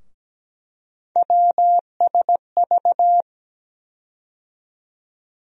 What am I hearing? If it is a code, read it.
Morse code, heard as a keyed tone, about 17 wpm, 700 Hz. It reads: WSV
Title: USA Call Sign Postfixes - 3 Letters 17wpm